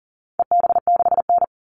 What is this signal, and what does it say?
Morse code, heard as a keyed tone, about 40 wpm, 700 Hz. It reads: EB6N
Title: All Cut Numbers - 4-Digit 40wpm